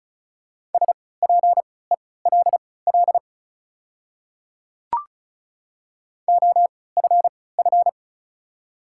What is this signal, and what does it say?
Morse code, heard as a keyed tone, about 34 wpm, spaced as for 20 wpm, 700 Hz. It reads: SPELL OFF
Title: 3x Character Spacing - Top 500 Words - 35wpm